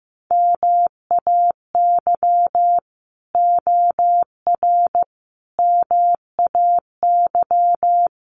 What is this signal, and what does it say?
Morse code, heard as a keyed tone, about 15 wpm, 700 Hz. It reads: MAY OR MAY
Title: Sets of 5 Words 15wpm